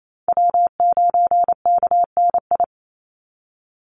Morse code, heard as a keyed tone, about 28 wpm, 700 Hz. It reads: W9KNI
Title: Call Signs - 28wpm